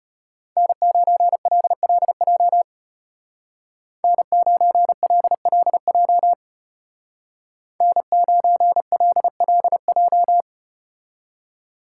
Morse code, heard as a keyed tone, about 33 wpm, 700 Hz. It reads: N9LLJ N9LLJ N9LLJ